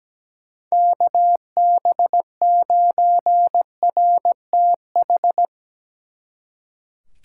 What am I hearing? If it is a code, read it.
Morse code, heard as a keyed tone, about 18 wpm, 700 Hz. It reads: KB9RTH
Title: Speed Racing - USA 2x3 Call Signs 17wpm